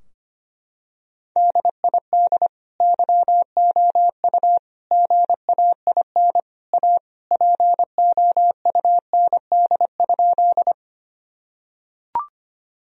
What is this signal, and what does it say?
Morse code, heard as a keyed tone, about 25 wpm, 700 Hz. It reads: DID YOU GAIN A POUND?
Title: Sentences from Top 1000 Words - Review Part 2 - 25wpm